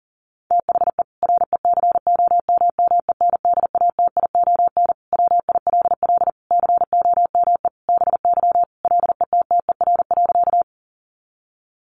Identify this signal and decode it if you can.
Morse code, heard as a keyed tone, about 37 wpm, 700 Hz. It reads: THE RECOMMENDATION WILL COME BY LETTER.
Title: Sentences from Top 1400 Words 40wpm